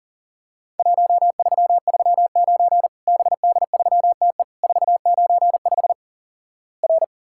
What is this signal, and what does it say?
Morse code, heard as a keyed tone, about 43 wpm, 700 Hz. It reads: 1339 BD3TE 495 R
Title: Worked All Europe DX Contest - QTC Exchange - 40wpm